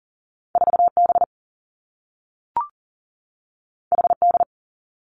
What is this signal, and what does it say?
Morse code, heard as a keyed tone, about 40 wpm, 700 Hz. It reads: 4B HD
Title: Two Challenging Letter-Number Combinations 40wpm